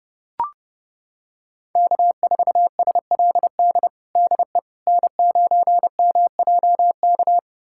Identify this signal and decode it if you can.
Morse code, heard as a keyed tone, about 30 wpm, 700 Hz. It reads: K4SLD DE N9MJK
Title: USA Callsign DE Callsign - 5 Characters - 30wpm